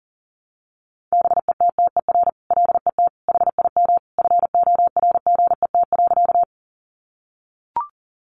Morse code, heard as a keyed tone, about 40 wpm, 700 Hz. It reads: BETTER LET HIM FORGET.
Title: Sentences from Top 1200 Words 40wpm